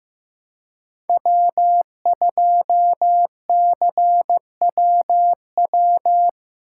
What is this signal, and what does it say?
Morse code, heard as a keyed tone, about 15 wpm, 700 Hz. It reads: W2CWW
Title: USA Call Signs - 5 Characters - 3x Character Spacing - 15wpm